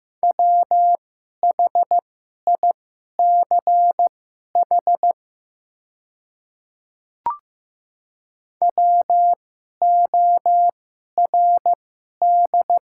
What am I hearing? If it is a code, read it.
Morse code, heard as a keyed tone, about 15 wpm, spaced as for 11 wpm, 700 Hz. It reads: WHICH WORD
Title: 2x Character Spacing - Top 100 Words - 15wpm